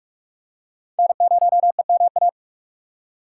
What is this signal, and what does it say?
Morse code, heard as a keyed tone, about 44 wpm, 690 Hz. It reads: N0EMA
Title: USA Call Signs - 5 Characters - 6x Character Spacing - 45wpm